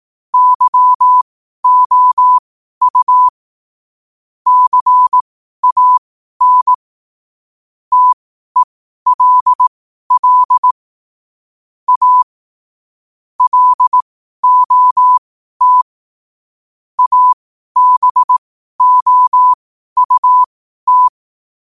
Morse code, heard as a keyed tone, about 18 wpm, spaced as for 12 wpm, 1000 Hz. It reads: YOU CAN TELL A LOT ABOUT
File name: qotdfiles / 12wpm